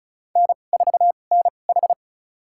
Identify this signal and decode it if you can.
Morse code, heard as a keyed tone, about 36 wpm, spaced as for 25 wpm, 700 Hz. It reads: N4NH
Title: USA Call Signs - 4 Characters - 2x Character Spacing - 35wpm